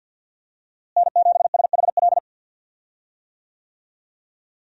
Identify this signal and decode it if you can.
Morse code, heard as a keyed tone, about 53 wpm, 700 Hz. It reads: N7SHL
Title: USA Call Signs - 5 Characters - Repeated 2x - 50wpm